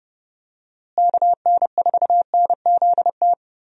Morse code, heard as a keyed tone, about 32 wpm, 700 Hz. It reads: KN4NZT
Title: USA Call Signs - 6 Characters - Repeated 3x - 30wpm